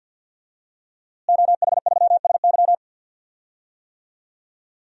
Morse code, heard as a keyed tone, about 52 wpm, 700 Hz. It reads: OH3SY
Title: DXCC-100 Call Signs - 5 Characters - Repeated 2x - 50wpm